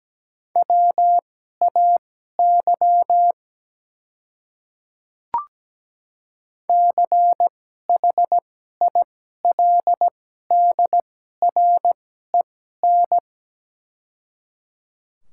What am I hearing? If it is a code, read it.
Morse code, heard as a keyed tone, about 17 wpm, spaced as for 12 wpm, 690 Hz. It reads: WAY CHILDREN
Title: 2x Character Spacing - Top 300 Words - 17wpm